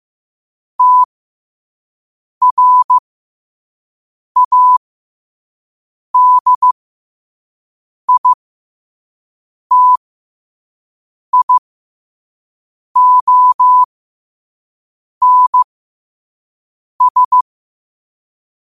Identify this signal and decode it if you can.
Morse code, heard as a keyed tone, about 15 wpm, spaced as for 5 wpm, 1000 Hz. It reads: TRADITIONS
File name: qotdfiles / 5wpm